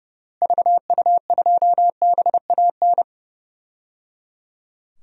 Morse code, heard as a keyed tone, about 30 wpm, 700 Hz. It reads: VU2BAN
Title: DXCC-100 Call Signs - 6 Characters - Repeated 3x - 30wpm